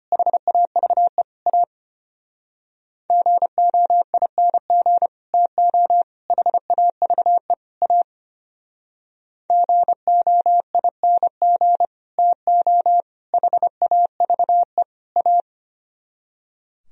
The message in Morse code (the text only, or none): HAVE A GOING TO HAVE A GOING TO HAVE A